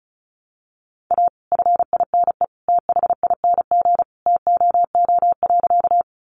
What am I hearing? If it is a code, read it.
Morse code, heard as a keyed tone, about 34 wpm, 700 Hz. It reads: A FINE THING TOO.
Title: Sentences from Top 500 Words 35wpm